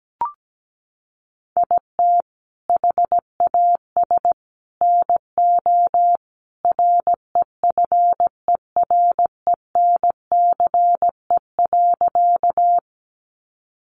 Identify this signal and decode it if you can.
Morse code, heard as a keyed tone, about 17 wpm, 700 Hz. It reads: IT HAS NO REFERENCE.
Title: Sentences from Top 1200 Words 17wpm